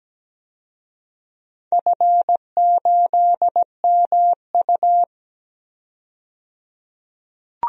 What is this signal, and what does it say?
Morse code, heard as a keyed tone, about 17 wpm, 700 Hz. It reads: F8MU
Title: DXCC-100 Call Signs - 4 Characters - 17wpm